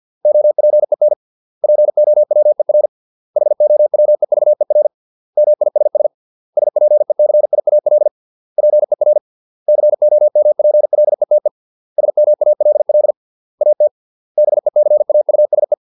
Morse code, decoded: OPEN POWER HOWEVER MISS SPECIAL PER COMPLETE SMALL AT BECAUSE

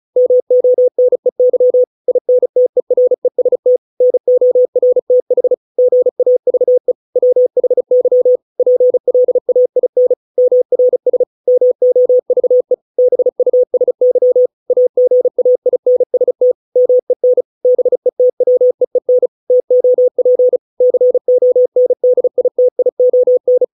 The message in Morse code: MONEY INTEREST NORTH GAVE WHY PLAIN MRS MOVE BUSY AGAINST MEN BETWEEN TOP CONDITION

